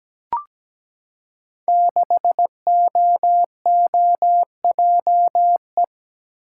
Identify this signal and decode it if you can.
Morse code, heard as a keyed tone, about 17 wpm, 700 Hz. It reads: 6OOJE
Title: Code Groups - 5 Letter:Numbers - 17wpm